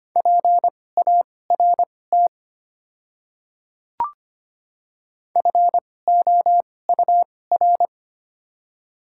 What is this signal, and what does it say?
Morse code, heard as a keyed tone, about 25 wpm, spaced as for 18 wpm, 700 Hz. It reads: PART FOUR
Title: 2x Character Spacing - Top 500 Words - 25wpm